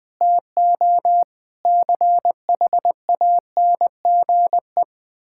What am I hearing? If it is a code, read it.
Morse code, heard as a keyed tone, about 20 wpm, 700 Hz. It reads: TO CHANGE
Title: Sentences from Top 1000 Words - Review Part 6 - 20wpm